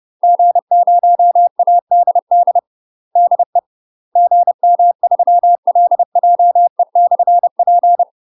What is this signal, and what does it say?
Morse code, heard as a keyed tone, about 30 wpm, 700 Hz. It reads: G0ADD DE GM3LJE/P
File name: UK MASTER QSOs TRAINING / 30WPM / UK MASTER QSOs 0017